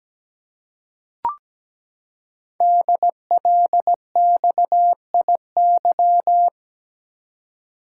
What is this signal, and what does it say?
Morse code, heard as a keyed tone, about 17 wpm, 700 Hz. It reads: DLXIY
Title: Code Groups - 5 Letters - 17wpm